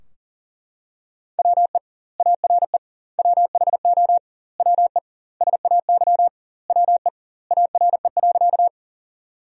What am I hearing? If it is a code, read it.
Morse code, heard as a keyed tone, about 40 wpm, 700 Hz. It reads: WE ARE WHO WE SAY WE ARE.
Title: 2x Word-Spacing - Sentences from Top 200 Words - 40wpm